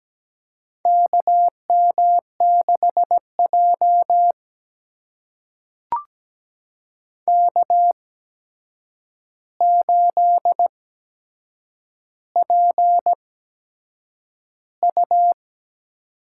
Morse code, heard as a keyed tone, about 17 wpm, spaced as for 7 wpm, 700 Hz. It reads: KM6J K8PU
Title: USA Call Signs - 4 Characters - 8x Character Spacing - 17wpm